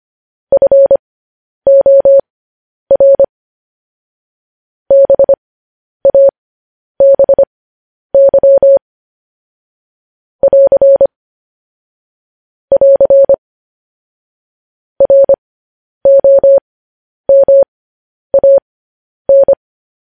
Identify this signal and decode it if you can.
Morse code, heard as a keyed tone, about 25 wpm, spaced as for 10 wpm, 550 Hz. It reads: FOR BABY <AR> <AR> ROMAN
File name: podcast / News headlines-1x-25x10